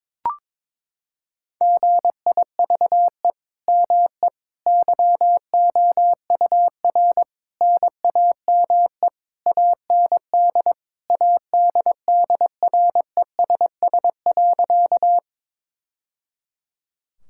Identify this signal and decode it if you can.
Morse code, heard as a keyed tone, about 22 wpm, 700 Hz. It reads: GIVE ME YOUR NAME AND ADDRESS.
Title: Sentences from Top 1000 Words - Review Part 6 - 22wpm